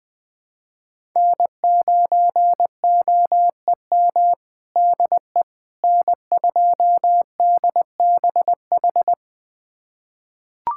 Morse code, heard as a keyed tone, about 21 wpm, 700 Hz. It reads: N9OEM DE N2DBH
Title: USA Callsign DE Callsign - 5 Characters - 20wpm